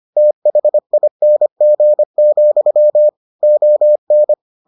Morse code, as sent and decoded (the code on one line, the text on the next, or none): - .... .. -. --. --..-- / --- -.
THING, ON